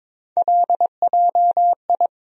Morse code, heard as a keyed tone, about 22 wpm, 700 Hz. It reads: LJI
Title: USA Call Sign Postfixes - 3 Letters 22wpm